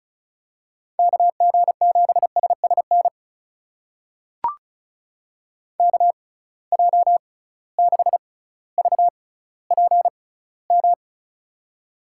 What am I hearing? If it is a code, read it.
Morse code, heard as a keyed tone, about 36 wpm, spaced as for 18 wpm, 700 Hz. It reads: KG7SSN KJ6VPM
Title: USA Call Signs - 6 Characters - 6x Character Spacing - 35wpm